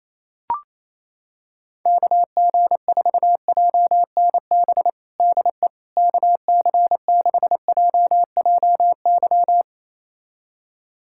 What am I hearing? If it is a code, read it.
Morse code, heard as a keyed tone, about 28 wpm, 700 Hz. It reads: KG4JNB DE KC6JJY
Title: USA Callsign DE Callsign 28wpm